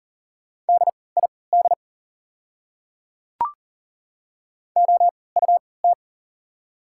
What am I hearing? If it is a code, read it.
Morse code, heard as a keyed tone, about 39 wpm, spaced as for 22 wpm, 690 Hz. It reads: DID OUT
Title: 3x Character Spacing - Top 100 Words - 40wpm